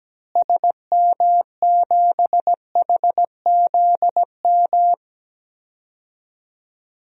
Morse code, heard as a keyed tone, about 17 wpm, 700 Hz. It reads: SM7HZM
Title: DXCC-100 Call Signs - 6 Characters - Repeated 3x - 17wpm